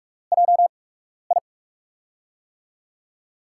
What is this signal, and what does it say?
Morse code, heard as a keyed tone, about 44 wpm, spaced as for 12 wpm, 700 Hz. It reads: JI
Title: USA Call Signs - 4 Characters - 8x Character Spacing - 45wpm